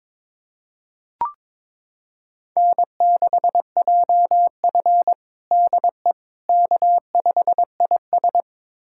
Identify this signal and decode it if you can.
Morse code, heard as a keyed tone, about 22 wpm, 700 Hz. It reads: N6JF DE K5IS
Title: USA Callsign DE Callsign - 4 Characters - 22wpm